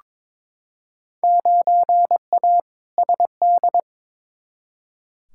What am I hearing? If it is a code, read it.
Morse code, heard as a keyed tone, about 22 wpm, 700 Hz. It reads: 9A SD
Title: Field Day Exchanges Encore - 22wpm